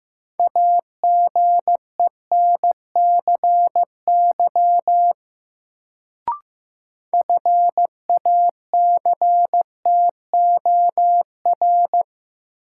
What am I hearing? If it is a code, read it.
Morse code, heard as a keyed tone, about 15 wpm, 700 Hz. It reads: AGENCY FACTOR
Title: Sentences from Top 1300 Words - New Word Drill - 15wpm